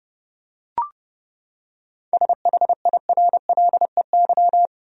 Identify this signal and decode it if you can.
Morse code, heard as a keyed tone, about 30 wpm, 700 Hz. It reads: SHIRLEY
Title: USA Common Names - 30wpm